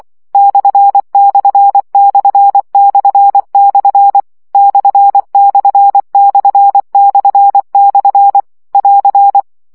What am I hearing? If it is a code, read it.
Morse code, heard as a keyed tone, about 24 wpm, 800 Hz. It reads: ///// ///// <AR>
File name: CW gyakorlas per 20 baud